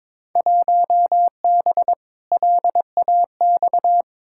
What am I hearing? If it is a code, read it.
Morse code, heard as a keyed tone, about 22 wpm, 700 Hz. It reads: 1B LAX
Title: Field Day Exchanges - 22wpm